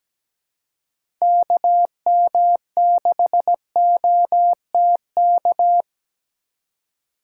Morse code, heard as a keyed tone, about 17 wpm, 700 Hz. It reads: KM6OTK
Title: USA Call Signs - 6 Characters - 2x Character Spacing - 17wpm